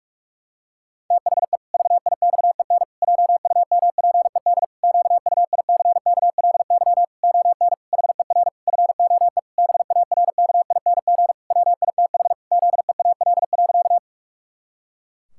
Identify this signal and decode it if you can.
Morse code, heard as a keyed tone, about 46 wpm, 700 Hz. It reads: THE VIXEN JUMPED QUICKLY ON HER FOE BARKING WITH ZEAL.
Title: Pangram Challenge v2 - 45wpm